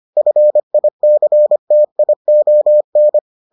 Morse code, heard as a keyed tone, about 25 wpm, 600 Hz. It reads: FICTION